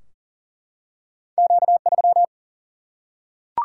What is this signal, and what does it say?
Morse code, heard as a keyed tone, about 40 wpm, 700 Hz. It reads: Q3
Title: Single-Letter Single-Number 40wpm